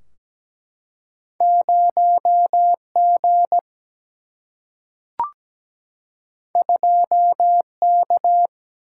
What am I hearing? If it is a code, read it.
Morse code, heard as a keyed tone, about 17 wpm, 700 Hz. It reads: 0G 2K
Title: Single-Letter Single-Number 17wpm